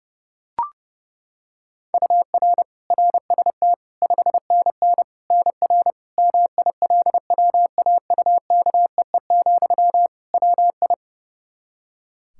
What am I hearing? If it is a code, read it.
Morse code, heard as a keyed tone, about 31 wpm, 700 Hz. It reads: UR RST 5NN NR MILWAUKEE, WI
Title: Practice QSO - Exchange 1 - Beginning - 30wpm